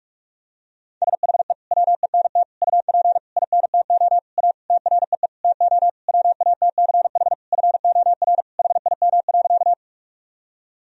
Morse code, heard as a keyed tone, about 46 wpm, 700 Hz. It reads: SHE WENT UP INTO A TREE TO WATCH FOR HIM.
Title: Sentences from Top 1000 Words - Review Part 1 - 45wpm